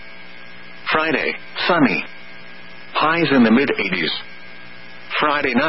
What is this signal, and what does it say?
Emergency Alert System (SAME) tones (header bursts), heard as harsh buzzing bursts, with speech. It is heard from 0.3 to 5.0 s.